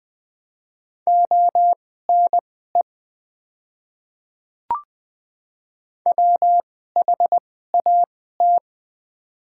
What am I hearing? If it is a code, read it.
Morse code, heard as a keyed tone, about 20 wpm, spaced as for 15 wpm, 700 Hz. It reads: ONE WHAT
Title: 2x Character Spacing - Top 100 Words - 20wpm